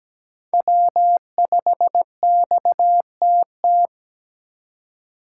Morse code, heard as a keyed tone, about 17 wpm, 700 Hz. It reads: W5XTT
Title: USA Call Signs - 5 Characters - 3x Character Spacing - 17wpm